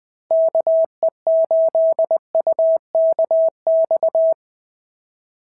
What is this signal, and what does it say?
Morse code, heard as a keyed tone, about 20 wpm, 690 Hz. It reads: KE8UKX
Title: POTA QSO Practice - 20wpm